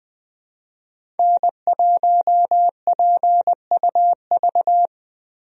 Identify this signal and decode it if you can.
Morse code, heard as a keyed tone, about 21 wpm, 700 Hz. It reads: N1PUV
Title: USA Call Signs - 5 Characters - Repeated 2x - 20wpm